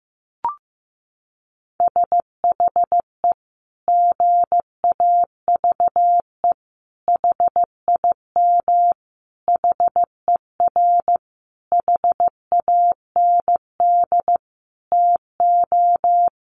Morse code, heard as a keyed tone, about 15 wpm, 700 Hz. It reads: SHE GAVE HIM HER HAND TO
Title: Sentences from Top 1400 Words 15wpm